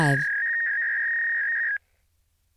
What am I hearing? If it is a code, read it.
Fax machine handshake, with speech.
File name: prev